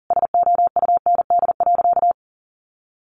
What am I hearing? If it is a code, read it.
Morse code, heard as a keyed tone, about 40 wpm, 700 Hz. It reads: SOUND.
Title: Sentences from Top 900 Words 40wpm